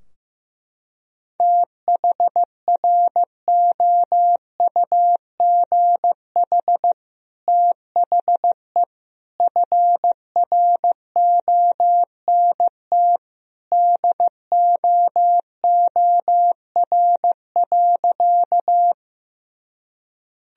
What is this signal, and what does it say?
Morse code, heard as a keyed tone, about 15 wpm, 700 Hz. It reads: THROUGH THE FRONT DOOR.
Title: Sentences from Top 1000 Words - Review Part 2 - 15wpm